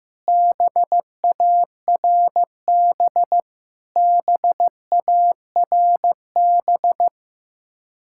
Morse code, heard as a keyed tone, about 15 wpm, 700 Hz. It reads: BARB BARB
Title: Practice QSO - Exchange 1 - Beginning - 15wpm